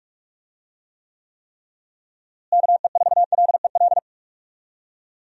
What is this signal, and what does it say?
Morse code, heard as a keyed tone, about 41 wpm, 700 Hz. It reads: KE4LEL
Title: USA Call Signs - 6 Characters - Repeated 3x - 45wpm